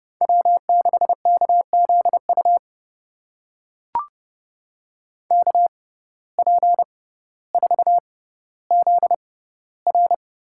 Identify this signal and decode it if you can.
Morse code, heard as a keyed tone, about 30 wpm, spaced as for 15 wpm, 700 Hz. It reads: W6KZU KP4ZR
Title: USA Call Signs - 5 Characters - 6x Character Spacing - 30wpm